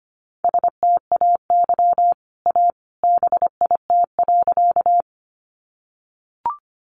Morse code, heard as a keyed tone, about 25 wpm, 700 Hz. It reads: STAY A BIT.
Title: Sentences from Top 900 Words 25wpm